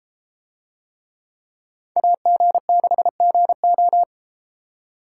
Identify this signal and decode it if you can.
Morse code, heard as a keyed tone, about 31 wpm, 700 Hz. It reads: AG6GO